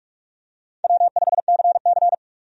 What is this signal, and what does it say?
Morse code, heard as a keyed tone, about 46 wpm, 700 Hz. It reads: W5CC
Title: USA Call Signs - 4 Characters - 45wpm